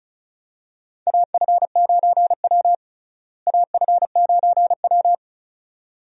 Morse code, heard as a keyed tone, about 36 wpm, 700 Hz. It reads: AF9W AF9W
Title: USA Call Signs - 4 Characters - Repeated 2x - 35wpm